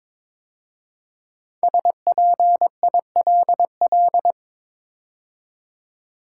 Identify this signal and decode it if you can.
Morse code, heard as a keyed tone, about 23 wpm, 700 Hz. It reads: SPILL
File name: 5 Letter Words - First 3 Letters Given - 22wpm